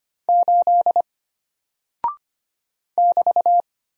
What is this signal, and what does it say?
Morse code, heard as a keyed tone, about 25 wpm, 700 Hz. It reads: 8 <BT>
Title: Single Character (Letter:Number:Punctuation:Prosign) Alt 25wpm